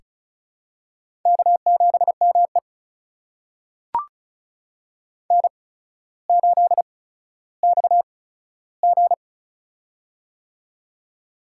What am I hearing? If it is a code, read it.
Morse code, heard as a keyed tone, about 36 wpm, spaced as for 15 wpm, 700 Hz. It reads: K7ME N8XG
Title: USA Call Signs - 4 Characters - 8x Character Spacing - 35wpm